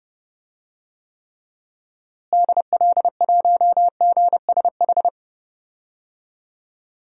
Morse code, heard as a keyed tone, about 30 wpm, 700 Hz. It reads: DL1GSH
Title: DXCC-100 Call Signs - 6 Characters - Repeated 2x - 30wpm